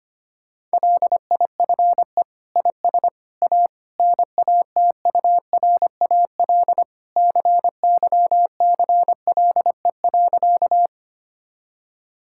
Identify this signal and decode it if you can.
Morse code, heard as a keyed tone, about 25 wpm, 700 Hz. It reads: LIFE IS A NATURAL CYCLE.
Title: Sentences from Top 1000 Words - Review Part 6 - 25wpm